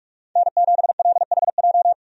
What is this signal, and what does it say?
Morse code, heard as a keyed tone, about 46 wpm, 700 Hz. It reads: N7LHJ
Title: USA Call Signs - 5 Characters - 45wpm